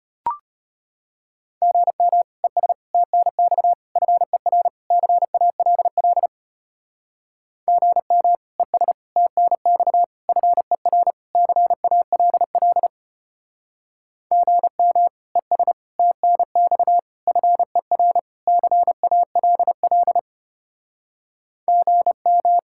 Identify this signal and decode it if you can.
Morse code, heard as a keyed tone, about 30 wpm, 700 Hz. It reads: GM ES TNX FER CALL GM ES TNX FER CALL GM ES TNX FER CALL GM
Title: Speed-Racing Common QSO Phrases - 25wpm